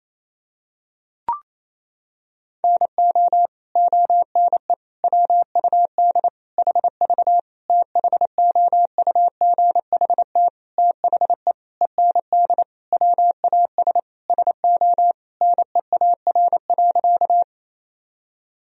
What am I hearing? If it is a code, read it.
Morse code, heard as a keyed tone, about 28 wpm, 700 Hz. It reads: NO ONE WUD HV THOUGHT THE END WAS SO NEAR.
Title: Sentences from Top 300 Words and Abbreviations 28wpm